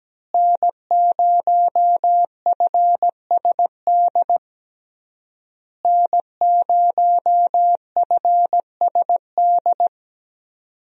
Morse code, heard as a keyed tone, about 17 wpm, 700 Hz. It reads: N0FSD N0FSD
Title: USA Call Signs - 5 Characters - Repeated 3x - 17wpm